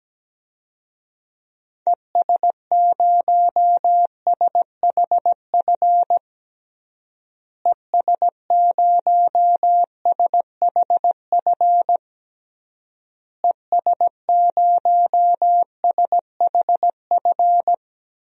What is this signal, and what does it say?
Morse code, heard as a keyed tone, about 17 wpm, 700 Hz. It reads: ES0SHF ES0SHF ES0SHF
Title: DXCC-100 Call Signs - 6 Characters - Repeated 3x - 17wpm